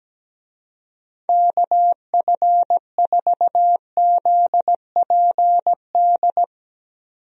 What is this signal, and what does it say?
Morse code, heard as a keyed tone, about 17 wpm, 700 Hz. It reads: KF4ZPD
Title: USA Call Signs - 6 Characters - 17wpm